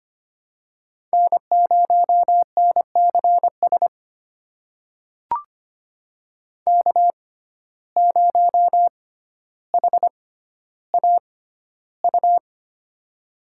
Morse code, heard as a keyed tone, about 25 wpm, spaced as for 13 wpm, 700 Hz. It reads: N0NCS K0HAU
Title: USA Call Signs - 5 Characters - 6x Character Spacing - 25wpm